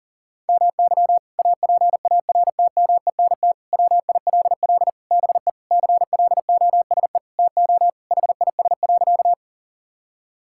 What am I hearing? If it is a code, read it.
Morse code, heard as a keyed tone, about 40 wpm, 700 Hz. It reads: MY APARTMENT WILL BE CLOSE TO HIS.
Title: Sentences from Top 1000 Words - Review Part 6 - 40wpm